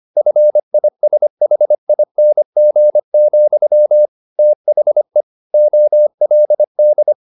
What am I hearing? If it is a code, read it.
Morse code, heard as a keyed tone, about 25 wpm, 600 Hz. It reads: FISHING, THE OLD
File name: the old man and the sea part 1 25wpm